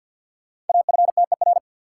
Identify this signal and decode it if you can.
Morse code, heard as a keyed tone, about 47 wpm, 700 Hz. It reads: AFTER
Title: Sentences from Top 200 Words - New Word Drill - 50wpm